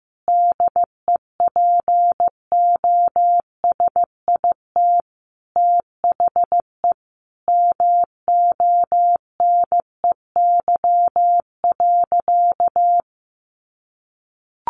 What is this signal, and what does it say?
Morse code, heard as a keyed tone, about 15 wpm, 700 Hz. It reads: DEPOSIT THE MONEY.
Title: Sentences from Top 1300 Words 15wpm